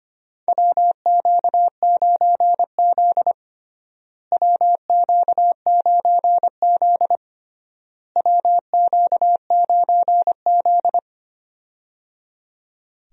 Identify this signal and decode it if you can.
Morse code, heard as a keyed tone, about 25 wpm, 700 Hz. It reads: WQ9Z WQ9Z WQ9Z
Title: USA Call Signs - 4 Characters - Repeated 3x - 25wpm